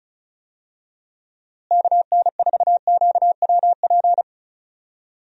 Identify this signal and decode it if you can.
Morse code, heard as a keyed tone, about 36 wpm, 700 Hz. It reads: KN4QWP
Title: USA Call Signs - 6 Characters - Repeated 3x - 35wpm